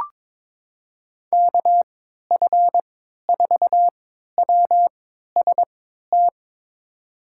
Morse code, heard as a keyed tone, about 22 wpm, spaced as for 12 wpm, 700 Hz. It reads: KF4WST
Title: USA Call Signs - 6 Characters - 3x Character Spacing - 22wpm